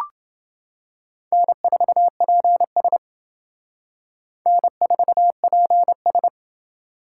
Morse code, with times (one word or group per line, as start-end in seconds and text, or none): 1.3-3.0 N4PS
4.5-6.3 N4PS